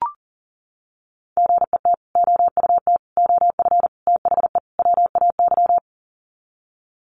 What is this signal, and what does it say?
Morse code, heard as a keyed tone, about 40 wpm, 700 Hz. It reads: GET OUT OF THE WAY
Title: Sets of 5 Words Encore 40wpm